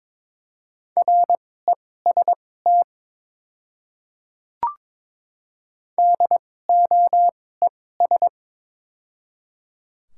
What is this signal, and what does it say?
Morse code, heard as a keyed tone, about 22 wpm, spaced as for 16 wpm, 700 Hz. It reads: REST DOES
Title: 2x Character Spacing - Top 500 Words - 22wpm